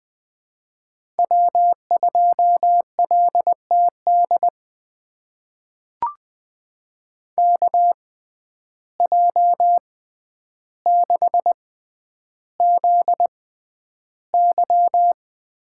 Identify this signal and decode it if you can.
Morse code, heard as a keyed tone, about 20 wpm, spaced as for 10 wpm, 700 Hz. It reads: W2LTD KJ6ZY
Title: USA Call Signs - 5 Characters - 6x Character Spacing - 20wpm